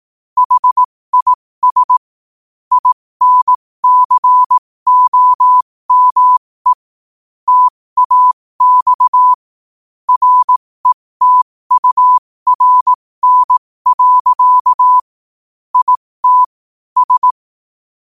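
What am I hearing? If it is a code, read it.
Morse code, heard as a keyed tone, about 18 wpm, spaced as for 15 wpm, 1000 Hz. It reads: HIS INCOME TAX RETURN. IT S
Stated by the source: Morse quote-of-the-day practice at 15 WPM from Morse Resource.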